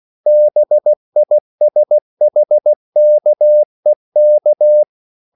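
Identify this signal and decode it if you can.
Morse code, heard as a keyed tone, about 16 wpm, 600 Hz. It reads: BISHKEK
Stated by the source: CW practice of national capitals at 16 words per minute, from ON6ZQ.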